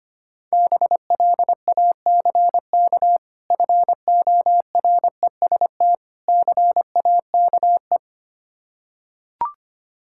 Morse code, heard as a keyed tone, about 25 wpm, 690 Hz. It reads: BLACK FOREST CAKE
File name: practice / 25wpm / Difficult Words with Hints - 25wpm